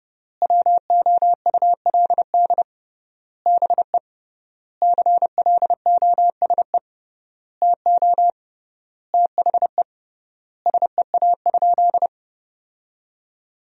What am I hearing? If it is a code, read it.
Morse code, heard as a keyed tone, about 30 wpm, 700 Hz. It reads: WOULD BE CLOSE TO THE SEA?
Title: 3x Word-Spacing - Sentences from Top 300 Words - 30wpm